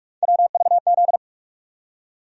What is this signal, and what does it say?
Morse code, heard as a keyed tone, about 46 wpm, 700 Hz. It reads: WVZ
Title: USA Call Sign Postfixes - 3 Letters 45wpm